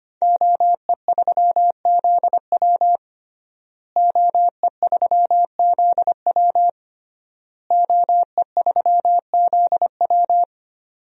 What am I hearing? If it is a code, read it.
Morse code, heard as a keyed tone, about 25 wpm, 700 Hz. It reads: OE3ZW OE3ZW OE3ZW
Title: DXCC-100 Call Signs - 5 Characters - Repeated 3x - 25wpm